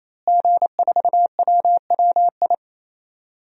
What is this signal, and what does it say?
Morse code, heard as a keyed tone, about 28 wpm, 700 Hz. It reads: G4WWI
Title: DXCC-100 Call Signs - 5 Characters - Repeated 3x - 28wpm